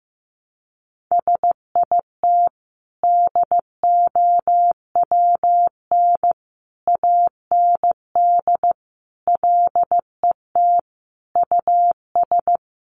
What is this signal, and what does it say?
Morse code, heard as a keyed tone, about 14 wpm, 700 Hz. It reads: SIT DOWN AND LET US
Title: Sentences from Top 1300 Words 15wpm